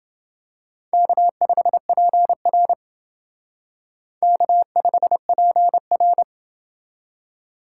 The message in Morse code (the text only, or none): K5PR K5PR